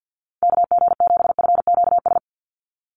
Morse code, heard as a keyed tone, about 50 wpm, 700 Hz. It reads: KG7FQS
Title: USA Call Signs 50wpm